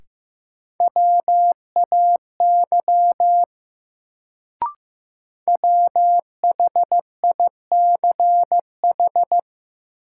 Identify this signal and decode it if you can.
Morse code, heard as a keyed tone, about 15 wpm, 700 Hz. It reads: WAY WHICH
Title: Sentences from Top 100 Words - New Word Drill - 15wpm